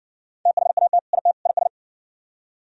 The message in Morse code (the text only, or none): THAT IT IS